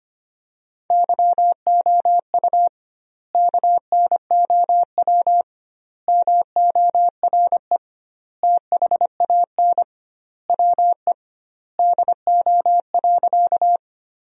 Morse code, heard as a keyed tone, about 25 wpm, 700 Hz. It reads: YOU KNOW MORE THAN WE DO.
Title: 2x Word-Spacing - Sentences from Top 100 Words - 25wpm